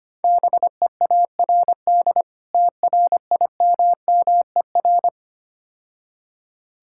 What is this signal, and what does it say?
Morse code, heard as a keyed tone, about 26 wpm, 700 Hz. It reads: BEARD TRIMMER
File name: practice / 25wpm / Difficult Words - Speed-Racing with Hints - 25wpm